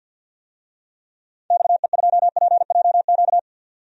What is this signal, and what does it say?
Morse code, heard as a keyed tone, about 53 wpm, 700 Hz. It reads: XE2PJQ
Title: DXCC-100 Call Signs - 6 Characters - 50wpm